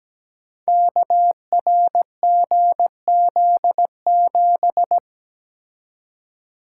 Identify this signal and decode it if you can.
Morse code, heard as a keyed tone, about 17 wpm, 700 Hz. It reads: KRGZ7
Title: Code Groups - 5 Letter:Numbers - 17wpm